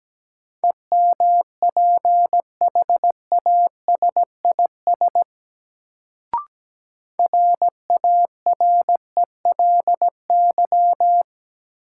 Morse code, heard as a keyed tone, about 17 wpm, 700 Hz. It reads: EMPHASIS RARELY
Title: Sentences from Top 1400 Words - New Word Drill Alt - 17wpm